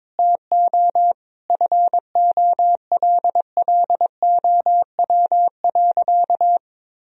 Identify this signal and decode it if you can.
Morse code, heard as a keyed tone, about 22 wpm, 700 Hz. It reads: TO FOLLOW.
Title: Sentences from Top 1000 Words - Review Part 2 - 22wpm